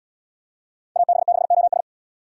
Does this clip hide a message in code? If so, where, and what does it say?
Yes: Morse code from 1.0 to 1.8 s, heard as a keyed tone, about 78 wpm, 700 Hz. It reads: AB6OS